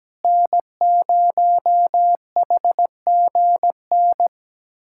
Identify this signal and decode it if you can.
Morse code, heard as a keyed tone, about 17 wpm, 700 Hz. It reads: N0HGN
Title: USA Callsign DE Callsign - 5 Characters - 17wpm